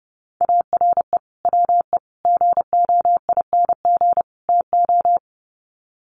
Morse code, heard as a keyed tone, about 30 wpm, 700 Hz. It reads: ARE WE GOING TO
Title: Sets of 4 Words 30wpm